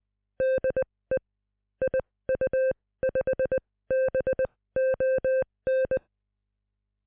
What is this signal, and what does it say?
Morse code, heard as a keyed tone, about 20 wpm, spaced as for 16 wpm, 520 Hz. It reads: DE IU5BON